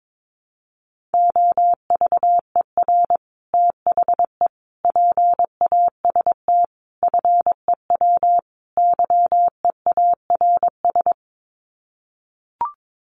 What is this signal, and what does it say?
Morse code, heard as a keyed tone, about 22 wpm, 700 Hz. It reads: OVER THE PAST FEW YEARS
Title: Sets of 5 Words Encore 22wpm